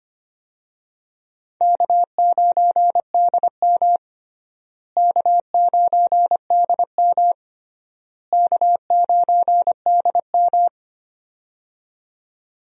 Morse code, heard as a keyed tone, about 25 wpm, 700 Hz. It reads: K9DM K9DM K9DM
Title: USA Call Signs - 4 Characters - Repeated 3x - 25wpm